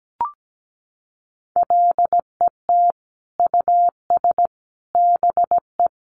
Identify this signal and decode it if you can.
Morse code, heard as a keyed tone, about 17 wpm, 700 Hz. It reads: LET US BE
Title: Sentences from Top 1400 Words 17wpm